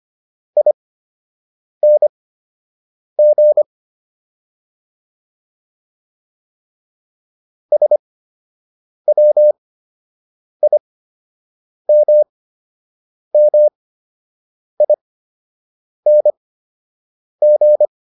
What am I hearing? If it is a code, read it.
Morse code, heard as a keyed tone, about 25 wpm, spaced as for 7 wpm, 600 Hz. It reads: ING SWIMMING